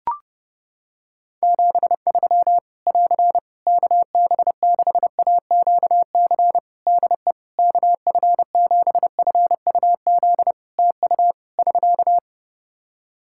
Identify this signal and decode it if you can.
Morse code, heard as a keyed tone, about 31 wpm, 700 Hz. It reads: 73 <AR> KB6AQC DE KF7FUZ TU <SK>
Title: Practice QSO - Exchange 4 - Ending - 30wpm